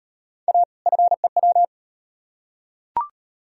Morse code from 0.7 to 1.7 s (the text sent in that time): FEW